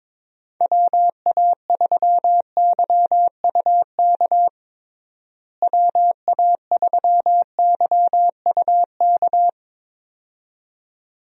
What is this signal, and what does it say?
Morse code, heard as a keyed tone, about 22 wpm, 700 Hz. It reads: WA3YUK WA3YUK
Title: USA Call Signs - 6 Characters - Repeated 2x - 22wpm